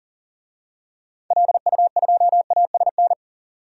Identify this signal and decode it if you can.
Morse code, heard as a keyed tone, about 40 wpm, 700 Hz. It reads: LU2ASN
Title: DXCC-100 Call Signs - 6 Characters - 40wpm